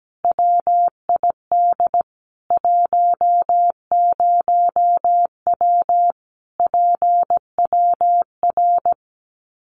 Morse code, heard as a keyed tone, about 17 wpm, 700 Hz. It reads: WID 10W PWR
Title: Common QSO Phrases 17wpm